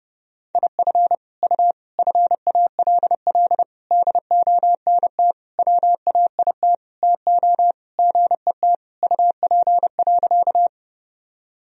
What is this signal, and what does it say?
Morse code, heard as a keyed tone, about 30 wpm, 700 Hz. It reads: IF U FALL DONT WAIT TO GET UP.
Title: Sentences from Top 500 Words and Abbreviations 30wpm